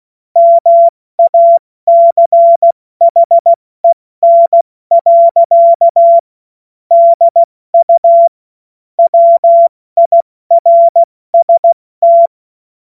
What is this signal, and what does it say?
Morse code, heard as a keyed tone, about 16 wpm, 680 Hz. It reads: MACHEN. DU WIRST
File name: Morsekurs / Witze / Witz 09. August 2022 70 F80